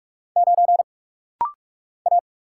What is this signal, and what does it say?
Morse code, heard as a keyed tone, about 46 wpm, 700 Hz. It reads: 9 A
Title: Single Letter-Number - Rapid-Fire 45wpm